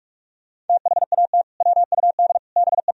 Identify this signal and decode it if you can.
Morse code, heard as a keyed tone, about 43 wpm, 700 Hz. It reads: THAT WUD BE
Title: Sentences from Top 400 Words and Abbreviations 45wpm